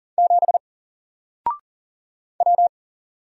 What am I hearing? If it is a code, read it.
Morse code, heard as a keyed tone, about 40 wpm, 700 Hz. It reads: 7 W
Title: Single Letter-Number 40wpm